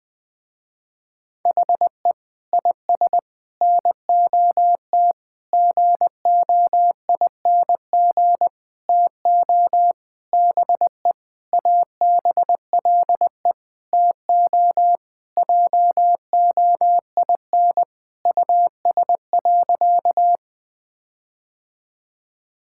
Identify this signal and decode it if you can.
Morse code, heard as a keyed tone, about 21 wpm, 700 Hz. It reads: HE IS NOT GOING TO BE ABLE TO JOIN US.
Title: Sentences with 5-Word Phrases - Spoken Beginning - 20wpm